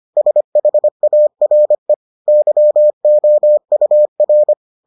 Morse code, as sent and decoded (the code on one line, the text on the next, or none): ... .... .- .-. / -.-- --- ..- .-.
SHARE YOUR